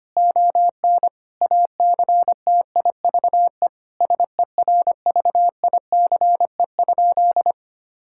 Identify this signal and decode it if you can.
Morse code, heard as a keyed tone, about 25 wpm, 700 Hz. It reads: ON ACTIVE SERVICE?
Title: Sentences from Top 1000 Words - Review Part 6 - 25wpm